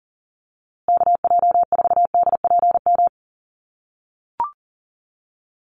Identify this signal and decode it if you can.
Morse code, heard as a keyed tone, about 40 wpm, 700 Hz. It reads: KJ4DPM
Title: USA Call Signs Repeated 2x 40wpm